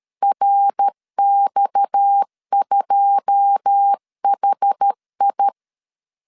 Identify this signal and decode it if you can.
Morse code, heard as a keyed tone, about 13 wpm, 800 Hz. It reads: RX2HI
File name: LEZIONI / 9 2 12wpm 2301